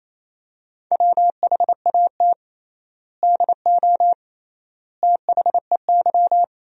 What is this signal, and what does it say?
Morse code, heard as a keyed tone, about 28 wpm, 700 Hz. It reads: WHAT DO THEY
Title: 3x Word-Spacing - Sentences from Top 200 Words - 28wpm